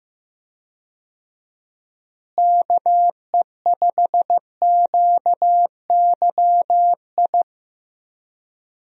Morse code, heard as a keyed tone, about 15 wpm, 700 Hz. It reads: KE5QYI
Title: USA Call Signs - 6 Characters - Repeated 2x - 15wpm